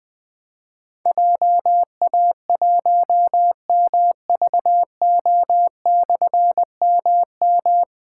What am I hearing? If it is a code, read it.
Morse code, heard as a keyed tone, about 20 wpm, 700 Hz. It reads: JA1MVO/MM
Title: Difficult Call Signs 20wpm